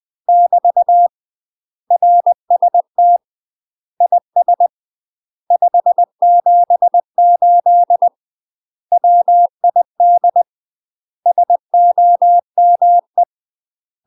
Morse code, heard as a keyed tone, about 20 wpm, 700 Hz. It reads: <BT> RST IS 578 WID SOME
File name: UK MASTER QSOs TRAINING / 20WPM / UK MASTER QSOs 0047